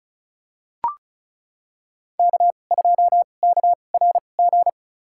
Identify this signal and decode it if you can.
Morse code, heard as a keyed tone, about 34 wpm, spaced as for 25 wpm, 700 Hz. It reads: K2KRG
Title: USA Call Signs - 2x Character Spacing - 35wpm